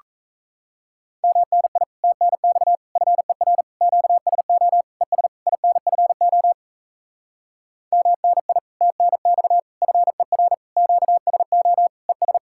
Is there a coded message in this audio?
Yes: Morse code from 1.2 to 12.4 s, heard as a keyed tone, about 37 wpm, 700 Hz. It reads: MNI TNX FER QSO ES INFO MNI TNX FER QSO ES